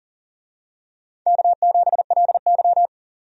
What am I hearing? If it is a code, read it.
Morse code, heard as a keyed tone, about 40 wpm, 700 Hz. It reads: K7LY
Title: USA Call Signs - 4 Characters - 40wpm